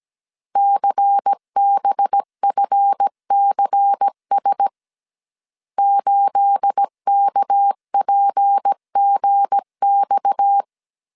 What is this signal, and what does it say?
Morse code, heard as a keyed tone, about 17 wpm, 800 Hz. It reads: CBFCS 8KPGX
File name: CW / DET / 1 9 1500C S A 16wpm 2828